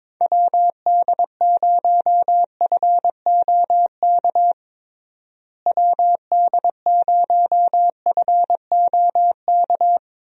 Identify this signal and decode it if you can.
Morse code, heard as a keyed tone, about 22 wpm, 700 Hz. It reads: WD0FOK WD0FOK
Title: USA Call Signs - 6 Characters - Repeated 3x - 22wpm